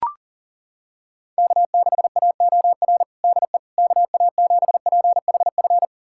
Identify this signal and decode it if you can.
Morse code, heard as a keyed tone, about 40 wpm, 700 Hz. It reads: K6AOR DE KA7PHF
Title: USA Callsign DE Callsign 40wpm